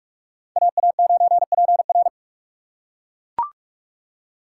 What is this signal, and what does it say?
Morse code, heard as a keyed tone, about 42 wpm, 700 Hz. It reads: AA9PR